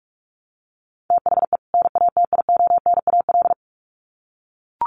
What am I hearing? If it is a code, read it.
Morse code, heard as a keyed tone, about 44 wpm, 700 Hz. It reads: THE NATIONAL
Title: Sets of 2 Words 45wpm